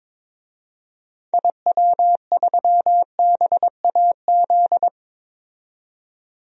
Morse code, heard as a keyed tone, about 21 wpm, 700 Hz. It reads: IW3BAZ
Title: DXCC-100 Call Signs - 6 Characters - 22wpm